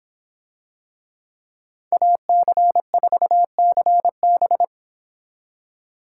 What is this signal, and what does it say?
Morse code, heard as a keyed tone, about 25 wpm, 700 Hz. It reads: AC4CB